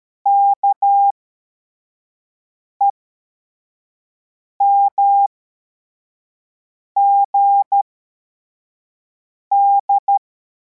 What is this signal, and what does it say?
Morse code, heard as a keyed tone, about 13 wpm, spaced as for 4 wpm, 800 Hz. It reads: KEMGD
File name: formation / cw / Text34